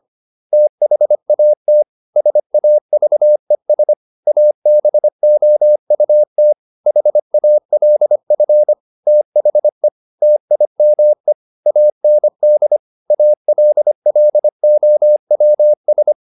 Morse code, heard as a keyed tone, about 25 wpm, 600 Hz. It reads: THAT SAVES ABOUT HALF THE TIME AND ALLOWS
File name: SS220 25